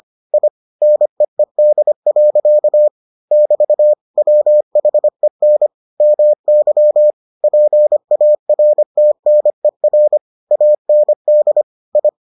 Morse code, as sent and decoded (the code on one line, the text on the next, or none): .. / -. . -.. .-.-.- / -...- / .-- .... -. / -- -.-- / .--. .- .-. - -. .-. / .- -. -.. / ..
I NEED. <BT> WHEN MY PARTNER AND I